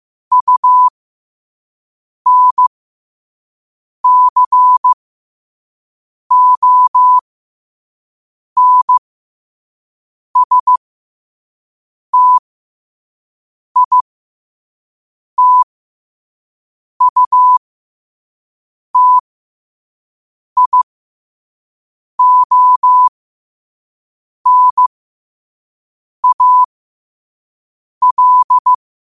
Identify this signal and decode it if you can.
Morse code, heard as a keyed tone, about 15 wpm, spaced as for 5 wpm, 1000 Hz. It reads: UNCONSTITUTIONAL
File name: qotdfiles / 5wpm